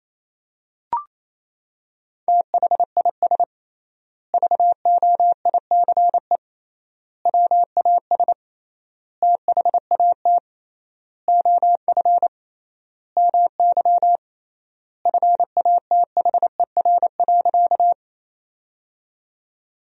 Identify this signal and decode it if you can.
Morse code, heard as a keyed tone, about 28 wpm, 700 Hz. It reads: THIS VOICE WAS THAT OF MY FATHER.
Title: 3x Word-Spacing - Sentences from Top 500 Words - 28wpm